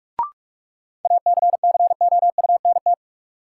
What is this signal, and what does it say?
Morse code, heard as a keyed tone, about 46 wpm, 700 Hz. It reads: ACCOUNT
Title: Sentences from Top 900 Words - New Word Drill - 45wpm